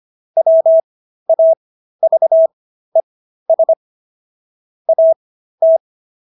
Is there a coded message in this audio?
Yes: Morse code from 0.4 to 5.8 s, heard as a keyed tone, about 25 wpm, spaced as for 13 wpm, 650 Hz. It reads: WAVES AT